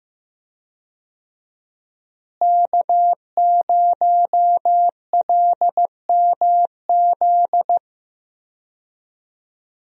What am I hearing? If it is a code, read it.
Morse code, heard as a keyed tone, about 16 wpm, 700 Hz. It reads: K0LMZ
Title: USA Call Signs - 5 Characters - Repeated 3x - 15wpm